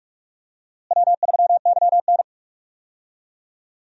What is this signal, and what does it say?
Morse code, heard as a keyed tone, about 46 wpm, 700 Hz. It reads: W3YN
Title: USA Call Signs - 4 Characters - Repeated 3x - 45wpm